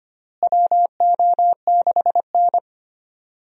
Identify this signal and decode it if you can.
Morse code, heard as a keyed tone, about 25 wpm, 700 Hz. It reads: WO6N
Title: USA Call Signs - 4 Characters - Repeated 2x - 25wpm